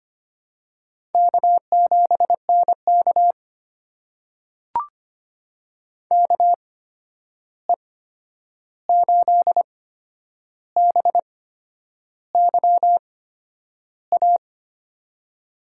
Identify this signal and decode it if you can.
Morse code, heard as a keyed tone, about 25 wpm, spaced as for 9 wpm, 700 Hz. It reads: K7NK KE8BYA
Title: USA Call Signs - 8x Character Spacing - 25wpm